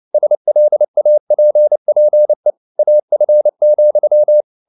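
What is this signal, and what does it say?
Morse code, heard as a keyed tone, about 29 wpm, 600 Hz. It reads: SLAPPE AF,
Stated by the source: Fast (QRQ) Morse at 29 wpm, from CW med Gnister.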